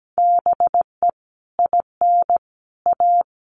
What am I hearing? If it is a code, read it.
Morse code, heard as a keyed tone, about 17 wpm, 700 Hz. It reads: BE IN A
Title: Sets of 5 Words Encore2 17wpm